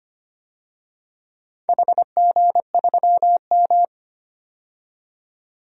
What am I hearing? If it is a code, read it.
Morse code, heard as a keyed tone, about 25 wpm, 690 Hz. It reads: HG3M
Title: DXCC-100 Call Signs - 4 Characters - 25wpm